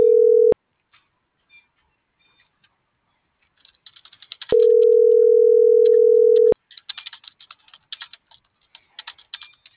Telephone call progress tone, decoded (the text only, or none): ringback tone